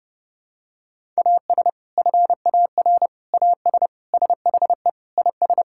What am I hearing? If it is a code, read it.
Morse code, heard as a keyed tone, about 30 wpm, 700 Hz. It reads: AS FAR AS SHE IS
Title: Sentences from Top 1000 Words - Review Part 6 - 30wpm